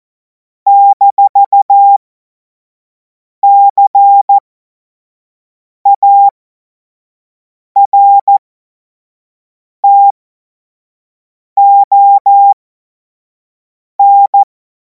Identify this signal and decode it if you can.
Morse code, heard as a keyed tone, about 14 wpm, spaced as for 5 wpm, 800 Hz. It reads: -CARTON
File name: Morsekurs / Witze / Witz 21. April 2026 25 F70